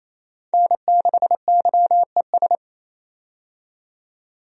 Morse code, heard as a keyed tone, about 28 wpm, 700 Hz. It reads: N6YES
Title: USA Call Signs - 5 Characters - 28wpm